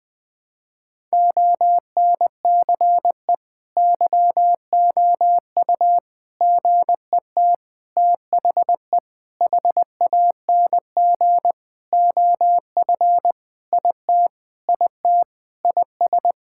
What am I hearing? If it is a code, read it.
Morse code, heard as a keyed tone, about 20 wpm, 700 Hz. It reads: ONCE YOU GET THE HANG OF IT IT IS
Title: Sentences with 7-Word Phrases - 20wpm